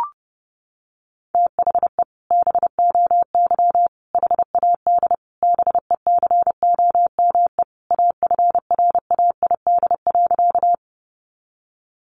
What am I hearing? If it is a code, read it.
Morse code, heard as a keyed tone, about 30 wpm, 700 Hz. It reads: THE BOY HAD BECOME AFRAID.
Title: Sentences from Top 1500 Words 30wpm